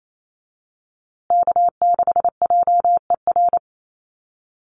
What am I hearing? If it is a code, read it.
Morse code, heard as a keyed tone, about 28 wpm, 700 Hz. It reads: K6JER
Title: USA Call Signs Repeated 3x 28wpm